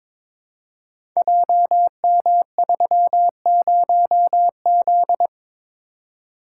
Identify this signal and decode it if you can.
Morse code, heard as a keyed tone, about 22 wpm, 690 Hz. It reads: JM30Z
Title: Code Groups - 5 Letter:Numbers - 22wpm